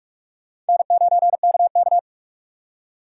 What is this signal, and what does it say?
Morse code, heard as a keyed tone, about 56 wpm, 700 Hz. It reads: N9KK
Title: Speed Racing - USA 1x2 Call Signs 45wpm